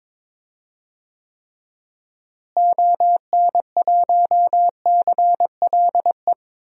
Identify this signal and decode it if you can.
Morse code, heard as a keyed tone, about 22 wpm, 700 Hz. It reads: ON1CLE
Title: DXCC-100 Call Signs - 6 Characters - Repeated 2x - 22wpm